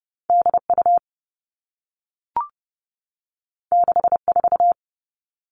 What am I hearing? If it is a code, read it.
Morse code, heard as a keyed tone, about 30 wpm, 700 Hz. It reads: DU 64
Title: All Cut Numbers - 2-Digit 30wpm